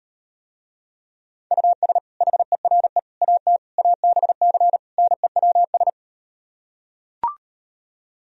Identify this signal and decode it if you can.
Morse code, heard as a keyed tone, about 38 wpm, 700 Hz. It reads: US HERE AT ABC NEWS